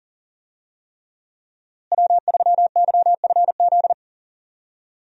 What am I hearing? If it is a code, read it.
Morse code, heard as a keyed tone, about 40 wpm, 700 Hz. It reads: W3YFZ
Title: USA Call Signs - 5 Characters - Repeated 3x - 40wpm